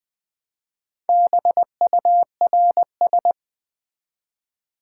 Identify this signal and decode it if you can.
Morse code, heard as a keyed tone, about 20 wpm, 700 Hz. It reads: BURS
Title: Build A Word - 1 Letter At A Time - 5-Letter Words - 20wpm